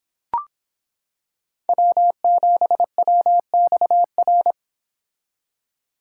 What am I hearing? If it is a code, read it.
Morse code, heard as a keyed tone, about 26 wpm, 700 Hz. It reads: W7WXR